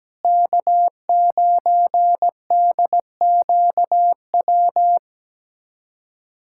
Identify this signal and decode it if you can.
Morse code, heard as a keyed tone, about 17 wpm, 700 Hz. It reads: K9DQW
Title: USA Callsign DE Callsign 17wpm